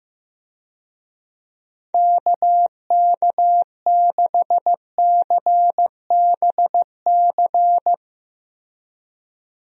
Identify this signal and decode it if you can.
Morse code, heard as a keyed tone, about 16 wpm, 700 Hz. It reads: KK6CBC
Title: USA Call Signs - 6 Characters - Repeated 3x - 15wpm